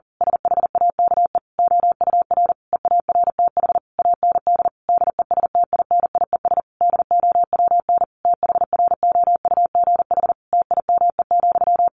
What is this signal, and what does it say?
Morse code, heard as a keyed tone, about 40 wpm, 700 Hz. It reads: SHAKE OUR EARTH AND DESTINIES DOWN THROUGH TIME,
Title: A Sound Of Thunder 40wpm